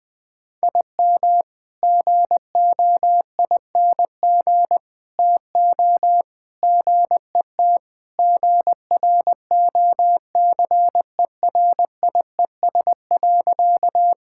Morse code, heard as a keyed tone, about 21 wpm, 700 Hz. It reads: IM GOING TO GET GROCERIES.
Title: Sentences with 3-Word Phrases - Spoken Beginning - 20wpm